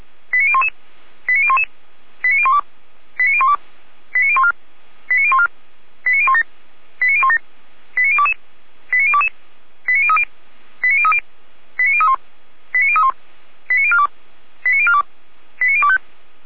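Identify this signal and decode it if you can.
Five-tone selective call (ZVEI), heard as quick runs of five tones, tones of about 70 ms. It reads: 89011 89011 89012 89012 89014 89014 89017 89017 89022 89022 89030 89030 89031 89031 89042 89042 89025
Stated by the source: ZVEI five-tone call received by SDR, from Feuerwehr Orlamünde.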